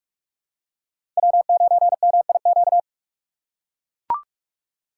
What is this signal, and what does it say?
Morse code, heard as a keyed tone, about 42 wpm, 700 Hz. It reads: W9MIQ
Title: USA Call Signs - 5 Characters - 3x Character Spacing - 45wpm